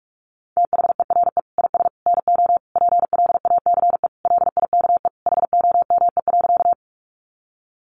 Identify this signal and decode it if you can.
Morse code, heard as a keyed tone, about 44 wpm, 700 Hz. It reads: THERE IS NO PLACE LIKE HOME.
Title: Sentences from Top 200 Words 45wpm